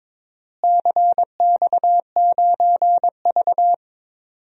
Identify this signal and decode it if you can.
Morse code, heard as a keyed tone, about 22 wpm, 700 Hz. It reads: CX9V
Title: DXCC-100 Call Signs - 4 Characters - Repeated 3x - 22wpm